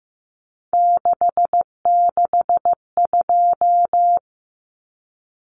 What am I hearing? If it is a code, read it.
Morse code, heard as a keyed tone, about 15 wpm, 700 Hz. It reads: 662
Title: Numbers - 3-Digit 15wpm